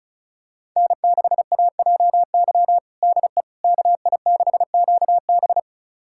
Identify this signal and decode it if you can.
Morse code, heard as a keyed tone, about 36 wpm, 700 Hz. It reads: N6AJY DE KI6QB
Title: USA Callsign DE Callsign - 5 Characters - 35wpm